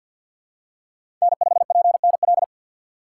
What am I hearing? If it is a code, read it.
Morse code, heard as a keyed tone, about 50 wpm, 700 Hz. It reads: N5PNL
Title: USA Call Signs - 5 Characters - 4x Character Spacing - 50wpm